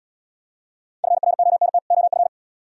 Morse code, heard as a keyed tone, about 79 wpm, 700 Hz. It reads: FRONT OF